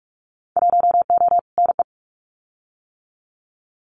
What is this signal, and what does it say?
Morse code, heard as a keyed tone, about 46 wpm, 700 Hz. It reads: 1O NE
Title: Winter Field Day Exchanges 45wpm